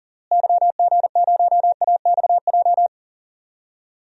Morse code, heard as a keyed tone, about 40 wpm, 700 Hz. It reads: YG0AXJ
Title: DXCC-100 Call Signs - 6 Characters - Repeated 3x - 40wpm